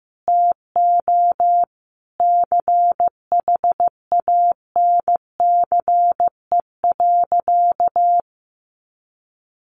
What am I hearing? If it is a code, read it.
Morse code, heard as a keyed tone, about 15 wpm, 700 Hz. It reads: TO CHANCE.
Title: Sentences from Top 1100 Words 15wpm